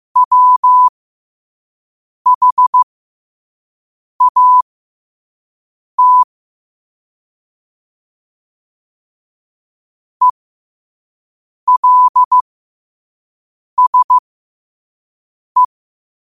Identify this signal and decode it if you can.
Morse code, heard as a keyed tone, about 15 wpm, spaced as for 5 wpm, 1000 Hz. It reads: WHAT ELSE
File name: qotdfiles / 5wpm